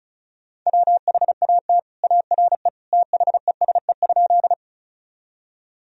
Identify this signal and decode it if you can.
Morse code, heard as a keyed tone, about 34 wpm, 700 Hz. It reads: WHAT ARE THESE?
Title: Sentences from Top 1000 Words - Review Part 1 - 35wpm